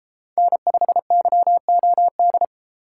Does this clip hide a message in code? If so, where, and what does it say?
Yes: Morse code from 0.4 to 2.4 s, heard as a keyed tone, about 31 wpm, 700 Hz. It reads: N5YOD